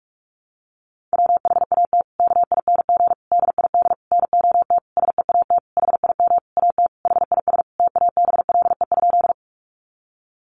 Morse code, heard as a keyed tone, about 44 wpm, 700 Hz. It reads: WHAT KING DID NOT SEAT HIM AT HIS TABLE?
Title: Sentences from Top 1500 Words 45wpm